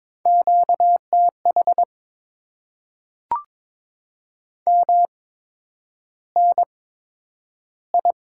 Morse code, heard as a keyed tone, about 22 wpm, spaced as for 9 wpm, 700 Hz. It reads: QTH MNI
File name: Top 90 QSO Elements - 8x Character Spacing - 22wpm